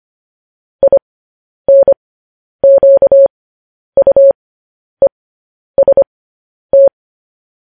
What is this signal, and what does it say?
Morse code, heard as a keyed tone, about 25 wpm, spaced as for 10 wpm, 550 Hz. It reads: INQUEST
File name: podcast / News headlines-1x-25x10